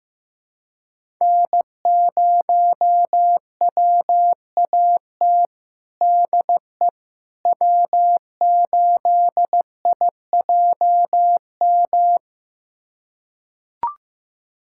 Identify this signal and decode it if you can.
Morse code, heard as a keyed tone, about 15 wpm, 700 Hz. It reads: N0WAT DE W8IJM
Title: USA Callsign DE Callsign - 5 Characters - 15wpm